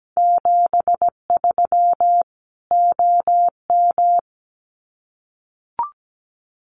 Morse code, heard as a keyed tone, about 17 wpm, 700 Hz. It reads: 73 OM
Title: Common QSO Phrases 17wpm